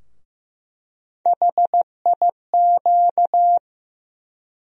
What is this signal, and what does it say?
Morse code, heard as a keyed tone, about 15 wpm, 700 Hz. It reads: HIQ
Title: USA Call Sign Postfixes - 3 Letters 15wpm